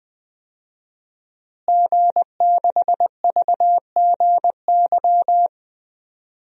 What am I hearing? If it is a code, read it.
Morse code, heard as a keyed tone, about 20 wpm, 700 Hz. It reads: G6VGY
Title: DXCC-100 Call Signs - 5 Characters - Repeated 2x - 20wpm